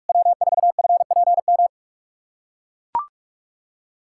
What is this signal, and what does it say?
Morse code, heard as a keyed tone, about 41 wpm, 700 Hz. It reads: W4FPM
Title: USA Callsign DE Callsign - 5 Characters - 45wpm